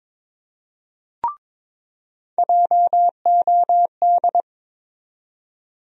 Morse code, heard as a keyed tone, about 21 wpm, 700 Hz. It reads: JOD
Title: USA Call Sign Postfixes - 3 Letters 22wpm